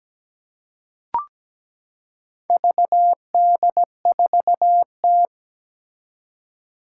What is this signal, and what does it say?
Morse code, heard as a keyed tone, about 17 wpm, 700 Hz. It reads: VD4T
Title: Four Challenging Letter-Number Combinations 17wpm